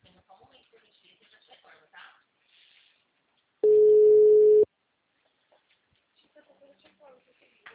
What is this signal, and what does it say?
Ringback tone (Europe, 425 Hz), with speech.
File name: out- -2013- - - .4759